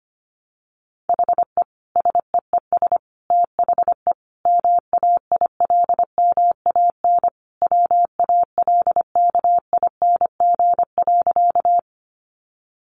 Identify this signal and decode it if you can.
Morse code, heard as a keyed tone, about 25 wpm, 700 Hz. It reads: HE SEES THE MAILMAN WALKING.
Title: Additional Easy Phrases 25wpm